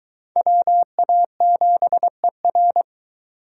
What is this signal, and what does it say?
Morse code, heard as a keyed tone, about 22 wpm, 700 Hz. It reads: WA7ER